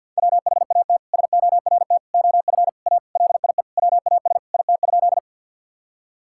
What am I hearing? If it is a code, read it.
Morse code, heard as a keyed tone, about 50 wpm, 700 Hz. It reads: WHAT SORT OF A LIE WAS IT?
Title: Sentences from Top 1600 Words 50wpm